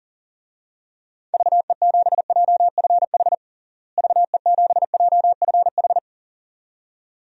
Morse code, heard as a keyed tone, about 40 wpm, 700 Hz. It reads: VE7JFH VE7JFH
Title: DXCC-100 Call Signs - 6 Characters - Repeated 2x - 40wpm